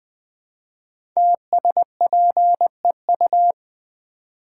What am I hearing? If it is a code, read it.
Morse code, heard as a keyed tone, about 20 wpm, 700 Hz. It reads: TSPEU
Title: Code Groups - 5 Letter:Numbers - 20wpm